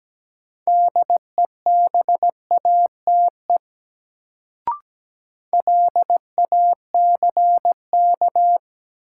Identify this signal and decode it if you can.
Morse code, heard as a keyed tone, about 17 wpm, 700 Hz. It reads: DEBATE LACK
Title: Sentences from Top 1100 Words - New Word Drill Alt - 17wpm